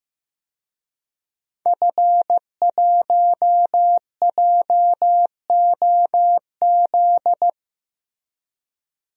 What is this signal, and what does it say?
Morse code, heard as a keyed tone, about 15 wpm, 700 Hz. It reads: F1JOZ
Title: DXCC-100 Call Signs - 5 Characters - Repeated 3x - 15wpm